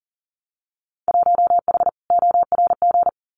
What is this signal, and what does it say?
Morse code, heard as a keyed tone, about 42 wpm, 700 Hz. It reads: 1H ORG
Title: Winter Field Day Exchanges 40wpm